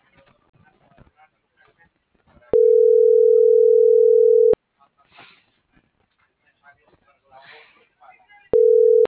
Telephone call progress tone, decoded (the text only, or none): ringback tone